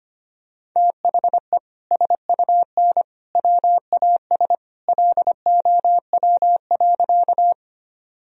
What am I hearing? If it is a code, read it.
Morse code, heard as a keyed tone, about 25 wpm, 700 Hz. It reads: THE SUN WAS LOW.
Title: Sentences from Top 1000 Words - Review Part 1 - 25wpm